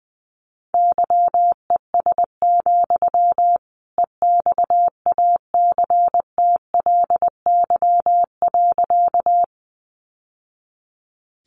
Morse code, heard as a keyed tone, about 20 wpm, 700 Hz. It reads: YES, EXACTLY.
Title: Sentences from Top 900 Words 20wpm